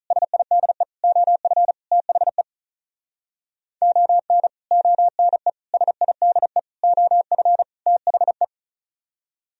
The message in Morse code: SIDE OF THE ON ONE SIDE OF THE

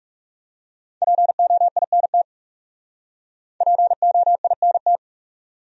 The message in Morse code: POINT POINT

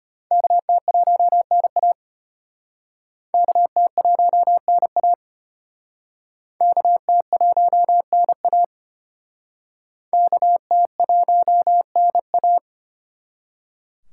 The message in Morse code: KT1NA KT1NA KT1NA KT1NA

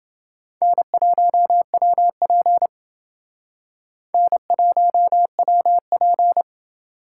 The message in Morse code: N1WP N1WP

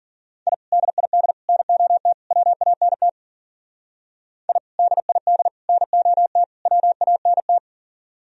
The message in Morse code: I DID NOT WANT I DID NOT WANT

